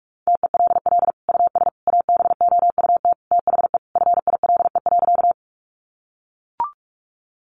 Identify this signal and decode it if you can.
Morse code, heard as a keyed tone, about 44 wpm, 700 Hz. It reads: TELL US ABOUT THE FILE.
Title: Sentences from Top 1200 Words 45wpm